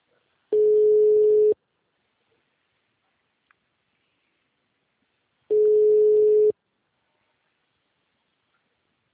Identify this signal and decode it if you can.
Ringback tone (Europe, 425 Hz).